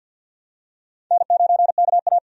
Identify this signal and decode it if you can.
Morse code, heard as a keyed tone, about 49 wpm, 700 Hz. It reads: N9KA
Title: USA Call Signs - 4 Characters - 50wpm